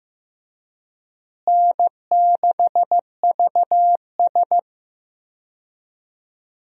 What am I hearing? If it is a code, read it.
Morse code, heard as a keyed tone, about 15 wpm, 700 Hz. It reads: N6VS
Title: USA Call Signs - 4 Characters - Repeated 3x - 15wpm